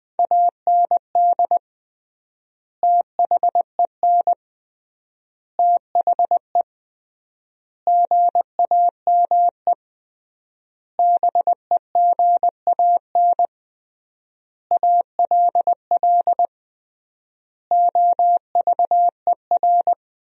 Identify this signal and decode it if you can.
Morse code, heard as a keyed tone, about 20 wpm, 700 Hz. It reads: AND THEN THE GAME BEGAN ALL OVER
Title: 3x Word-Spacing - Sentences from Top 500 Words - 20wpm